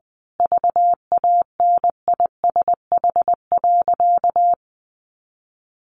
Morse code, heard as a keyed tone, about 20 wpm, 700 Hz. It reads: VANISH.
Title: A Sound Of Thunder 20wpm